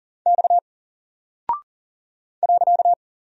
Morse code, heard as a keyed tone, about 40 wpm, 700 Hz. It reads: X .
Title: Single Character (Letter:Number:Punctuation:Prosign) Alt 40wpm